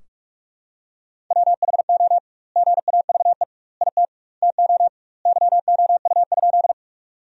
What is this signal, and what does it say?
Morse code, heard as a keyed tone, about 44 wpm, 700 Hz. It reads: WHO GAVE IT TO YOU?
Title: 2x Word-Spacing - Sentences from Top 500 Words - 45wpm